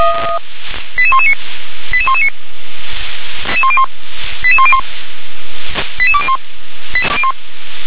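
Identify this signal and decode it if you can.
Five-tone selective call (ZVEI), heard as quick runs of five tones, tones of about 70 ms. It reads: 88119 88119 88101 88201
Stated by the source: ZVEI five-tone selective call received on an SDR.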